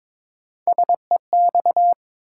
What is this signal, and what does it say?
Morse code, heard as a keyed tone, about 22 wpm, 700 Hz. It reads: SEX
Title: Sentences from Top 1600 Words - New Word Drill - 22wpm